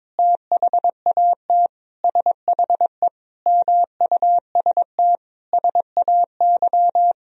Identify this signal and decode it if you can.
Morse code, heard as a keyed tone, about 22 wpm, 700 Hz. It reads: THAT SHE MUST SAY
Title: Sentences from Top 1000 Words - Review Part 3 - 22wpm